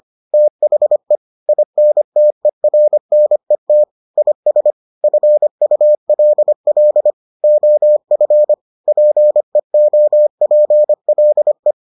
Morse code, heard as a keyed tone, about 25 wpm, 600 Hz. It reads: THE INTERNET IS FULL OF PEOPLE